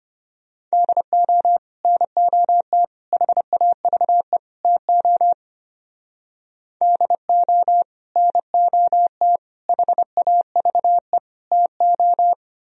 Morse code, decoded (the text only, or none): DO NOT HAVE TO DO NOT HAVE TO